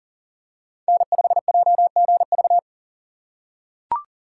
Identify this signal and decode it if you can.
Morse code, heard as a keyed tone, about 40 wpm, 700 Hz. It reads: N5JGV
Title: USA Call Signs - 8x Character Spacing - 40wpm